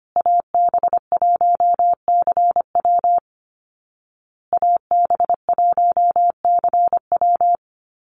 Morse code, heard as a keyed tone, about 25 wpm, 700 Hz. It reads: AB1CW AB1CW
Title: Call Signs Repeated 2x - 25wpm